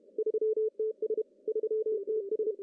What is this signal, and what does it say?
Morse code, heard as a keyed tone, about 28 wpm, 410 Hz. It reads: 3TS 3TS
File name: CW / cqww-cw-k5tr